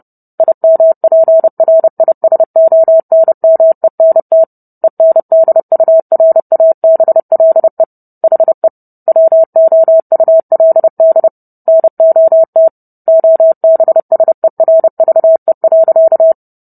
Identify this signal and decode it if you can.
Morse code, heard as a keyed tone, about 30 wpm, 650 Hz. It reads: IMPRISONMENT ENDURABLE HE WOULD NOT OBSERVE.